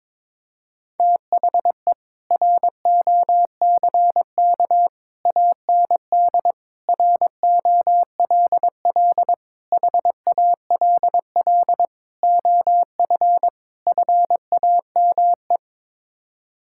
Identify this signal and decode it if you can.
Morse code, heard as a keyed tone, about 22 wpm, 700 Hz. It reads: THE ROCK AND ROLL HALL OF FAME
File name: practice / 22wpm / Sets of 7 Words - 22wpm